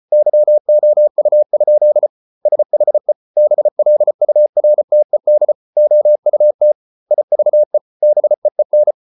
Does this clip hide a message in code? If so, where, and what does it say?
Yes: Morse code from 0.1 to 8.9 s, heard as a keyed tone, about 34 wpm, 600 Hz. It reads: YOU? SHE BLURTED OUT IVE BEEN